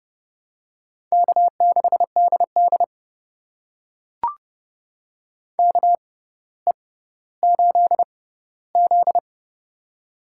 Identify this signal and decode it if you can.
Morse code, heard as a keyed tone, about 30 wpm, spaced as for 15 wpm, 700 Hz. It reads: K6DD KE8Z
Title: USA Call Signs - 6x Character Spacing - 30wpm